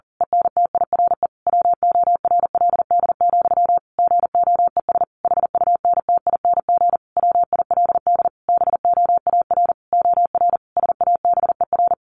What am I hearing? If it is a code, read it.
Morse code, heard as a keyed tone, about 40 wpm, 700 Hz. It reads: ENTIRE WORLD, GOES HUNTING WILD BOAR OR SABER
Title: A Sound Of Thunder 40wpm